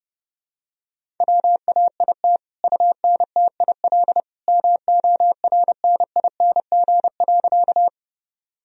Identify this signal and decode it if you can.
Morse code, heard as a keyed tone, about 32 wpm, 700 Hz. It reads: WAIT UNTIL MORNING.
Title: Sentences from Top 1000 Words - Review Part 1 - 30wpm